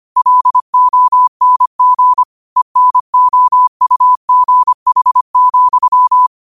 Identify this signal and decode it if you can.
Morse code, heard as a keyed tone, about 25 wpm, 1000 Hz. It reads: LONG ENOUGH,
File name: 25wpm